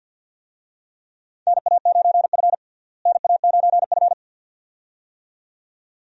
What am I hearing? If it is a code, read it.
Morse code, heard as a keyed tone, about 51 wpm, 700 Hz. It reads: NA9F NA9F
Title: USA Call Signs - 4 Characters - Repeated 2x - 50wpm